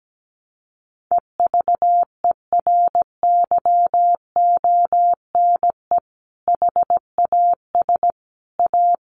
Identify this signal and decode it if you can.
Morse code, heard as a keyed tone, about 17 wpm, 700 Hz. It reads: EVERYONE HAS A
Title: Sentences from Top 1200 Words 17wpm